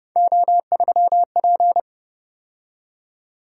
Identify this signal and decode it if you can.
Morse code, heard as a keyed tone, about 29 wpm, 700 Hz. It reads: O3P
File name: Speed-Racing - Three Letter-Number Combinations - 25wpm